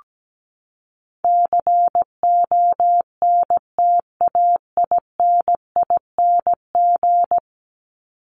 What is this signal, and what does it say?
Morse code, heard as a keyed tone, about 17 wpm, 700 Hz. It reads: CONTAINING
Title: ING Suffix - Top 100 Words with Base Words 17wpm